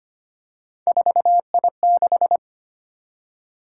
Morse code, heard as a keyed tone, about 25 wpm, 700 Hz. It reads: 4I6
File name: Three Letter-Number Combinations - 25wpm